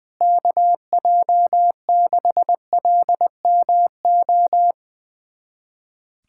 Morse code, heard as a keyed tone, about 20 wpm, 700 Hz. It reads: KJ6LMO
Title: USA Callsign DE Callsign - 6 Characters - 20wpm